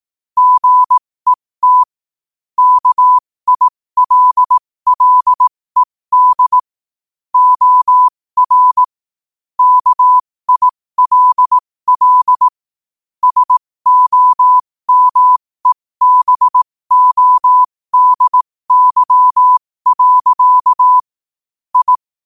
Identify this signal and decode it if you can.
Morse code, heard as a keyed tone, about 18 wpm, spaced as for 15 wpm, 1000 Hz. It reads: GET KILLED OR KILL SOMEBODY. I
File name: qotdfiles / 15wpm